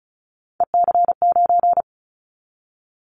Morse code, heard as a keyed tone, about 34 wpm, 700 Hz. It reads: EC9
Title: Callsign Prefixes for Top 100 DXCC - 35wpm